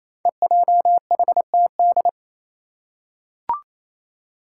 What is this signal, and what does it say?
Morse code, heard as a keyed tone, about 28 wpm, 700 Hz. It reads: EJHTD
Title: Five Challenging Letter-Number Combinations 28wpm